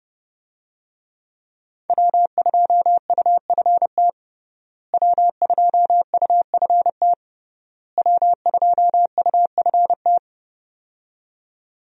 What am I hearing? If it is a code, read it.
Morse code, heard as a keyed tone, about 30 wpm, 700 Hz. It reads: W2UFT W2UFT W2UFT
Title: USA Call Signs - 5 Characters - Repeated 3x - 30wpm